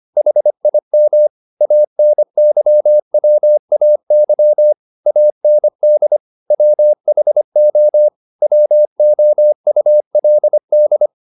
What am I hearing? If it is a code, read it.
Morse code, heard as a keyed tone, about 25 wpm, 600 Hz. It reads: HIM ANYWAY AND WHO WOULD
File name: the old man and the sea part 1 25wpm